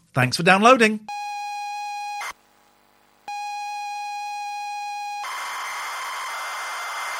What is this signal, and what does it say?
ZX Spectrum tape loading, with speech.